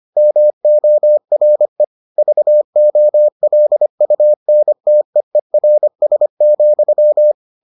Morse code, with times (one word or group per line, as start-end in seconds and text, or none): 0.2-1.8 MORE
2.2-7.3 VOLUNTEERS,